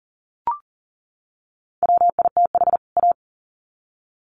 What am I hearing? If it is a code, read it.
Morse code, heard as a keyed tone, about 40 wpm, 700 Hz. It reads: WITH A
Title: Sets of 2 Words 40wpm